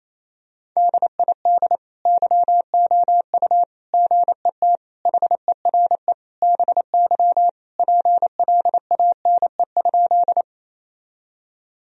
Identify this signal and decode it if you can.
Morse code, heard as a keyed tone, about 28 wpm, 700 Hz. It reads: DID YOU GET HERE BY PLANE?
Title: Sentences from Top 1000 Words - Review Part 3 - 28wpm